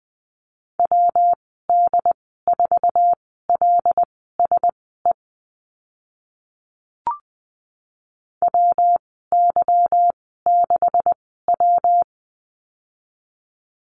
Morse code, heard as a keyed tone, about 20 wpm, spaced as for 14 wpm, 700 Hz. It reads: WD4LSE WY6W
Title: 161 Lesson 26g - Letter g - Call Signs 20wpm